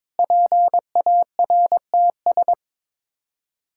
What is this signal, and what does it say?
Morse code, heard as a keyed tone, about 22 wpm, 700 Hz. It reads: PARTS
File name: practice / 22wpm / Sets of 7 Words - 22wpm